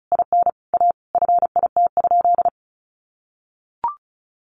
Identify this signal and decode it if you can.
Morse code, heard as a keyed tone, about 34 wpm, 700 Hz. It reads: IN A FIT?
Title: Sentences from Top 800 Words 35wpm